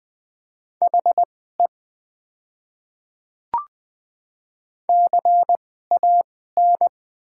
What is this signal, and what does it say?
Morse code, heard as a keyed tone, about 20 wpm, spaced as for 15 wpm, 690 Hz. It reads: HE CAN
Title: 2x Character Spacing - Top 200 Words - 20wpm